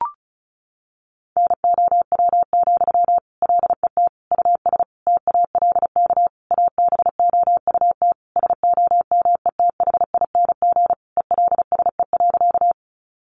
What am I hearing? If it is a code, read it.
Morse code, heard as a keyed tone, about 34 wpm, 700 Hz. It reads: NOW, LET US TALK ABOUT SOMETHING ELSE.
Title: Sentences from Top 800 Words 35wpm